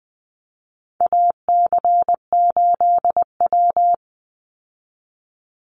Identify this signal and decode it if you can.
Morse code, heard as a keyed tone, about 20 wpm, 700 Hz. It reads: AC8W
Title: Call Signs Repeated 2x - 20wpm